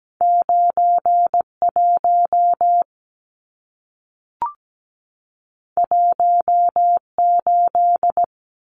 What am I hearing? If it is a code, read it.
Morse code, heard as a keyed tone, about 17 wpm, 700 Hz. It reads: 91 18
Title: Numbers - 2-Digit 17wpm